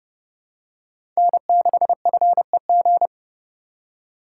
Morse code, heard as a keyed tone, about 30 wpm, 700 Hz. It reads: N6FEG
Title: USA Call Signs - 5 Characters - 4x Character Spacing - 30wpm